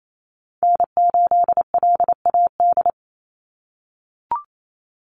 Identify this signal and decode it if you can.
Morse code, heard as a keyed tone, about 28 wpm, 700 Hz. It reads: N8LAD
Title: USA Call Signs 28wpm